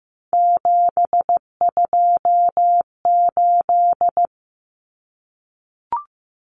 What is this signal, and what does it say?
Morse code, heard as a keyed tone, about 15 wpm, 700 Hz. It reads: 728
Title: Common Cut Numbers - 3-Digit 15wpm